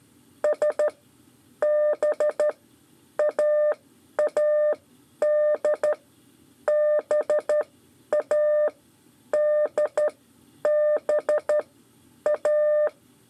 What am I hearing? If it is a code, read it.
Morse code, heard as a keyed tone, about 12 wpm, spaced as for 9 wpm, 600 Hz. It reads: SBAAD BADBA
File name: 5 LEZIONE RIPASSO 2